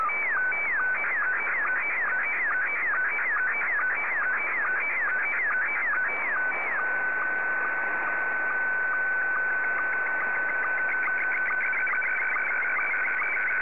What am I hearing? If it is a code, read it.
SSTV (slow-scan television), Scottie 1.